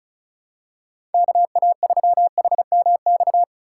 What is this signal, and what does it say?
Morse code, heard as a keyed tone, about 34 wpm, 700 Hz. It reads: KA3HMX
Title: USA Call Signs - 4x Character Spacing - 35wpm